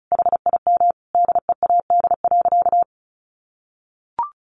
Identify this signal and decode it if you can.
Morse code, heard as a keyed tone, about 34 wpm, 700 Hz. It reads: HIM DEAD.
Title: Sentences from Top 1500 Words 35wpm